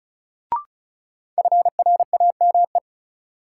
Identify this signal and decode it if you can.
Morse code, heard as a keyed tone, about 36 wpm, 700 Hz. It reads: FRAME
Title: Sentences from Top 1400 Words - New Word Drill Alt - 35wpm